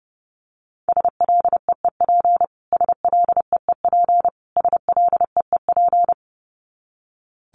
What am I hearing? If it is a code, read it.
Morse code, heard as a keyed tone, about 30 wpm, 700 Hz. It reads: SLEEP SLEEP SLEEP
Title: Top 500 Words Repeated 30wpm